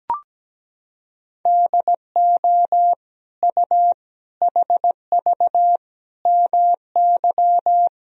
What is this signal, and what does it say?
Morse code, heard as a keyed tone, about 17 wpm, 700 Hz. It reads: DO U HV MY
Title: Sentences from Top 500 Words and Abbreviations 17wpm